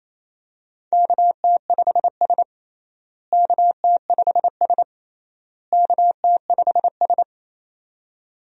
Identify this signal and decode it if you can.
Morse code, heard as a keyed tone, about 28 wpm, 700 Hz. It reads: KT5S KT5S KT5S
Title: USA Call Signs - 4 Characters - Repeated 3x - 28wpm